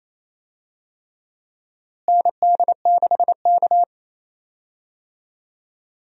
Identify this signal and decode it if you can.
Morse code, heard as a keyed tone, about 28 wpm, 700 Hz. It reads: ND6K
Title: USA Call Signs - 4 Characters - Repeated 3x - 28wpm